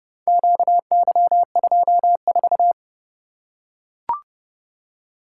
Morse code, heard as a keyed tone, about 29 wpm, 700 Hz. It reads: QY24